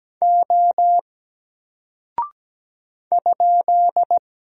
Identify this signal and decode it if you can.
Morse code, heard as a keyed tone, about 17 wpm, 700 Hz. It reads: O ?
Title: Single Character (Letter:Number:Punctuation:Prosign) Alt 17wpm